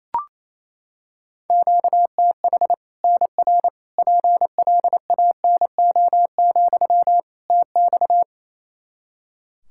Morse code, heard as a keyed tone, about 28 wpm, 700 Hz. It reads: QTH NR PLANO, TX
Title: Practice QSO - Exchange 1 - Beginning - 28wpm